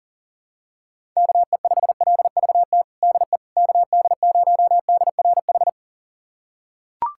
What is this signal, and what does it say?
Morse code, heard as a keyed tone, about 37 wpm, 700 Hz. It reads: KE5LVT DE KD0DRH
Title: USA Callsign DE Callsign - 6 Characters - 40wpm